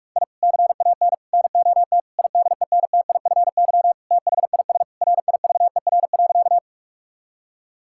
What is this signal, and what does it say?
Morse code, heard as a keyed tone, about 46 wpm, 700 Hz. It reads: I CAN NOT IDENTIFY THIS RIVER.
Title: Sentences from Top 1000 Words - Review Part 6 - 45wpm